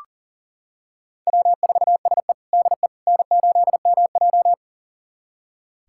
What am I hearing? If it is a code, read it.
Morse code, heard as a keyed tone, about 40 wpm, 700 Hz. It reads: W4SE DE N8MJ
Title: USA Callsign DE Callsign - 4 Characters - 40wpm